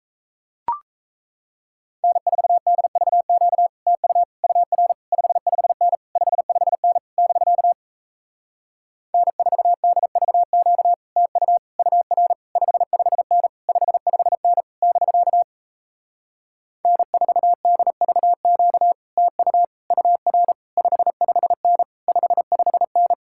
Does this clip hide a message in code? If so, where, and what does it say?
Yes: Morse code from 2.0 to 23.1 s, heard as a keyed tone, about 38 wpm, 700 Hz. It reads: N4DVQ TU UR 55N 55N <BK> N4DVQ TU UR 55N 55N <BK> N4DVQ TU UR 55N 55N